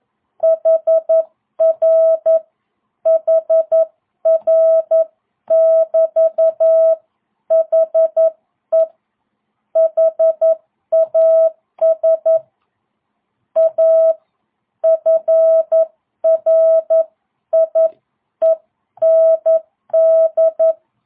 Morse code, heard as a keyed tone, about 11 wpm, 650 Hz. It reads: HR HR<BT>HE HAS A FRIEND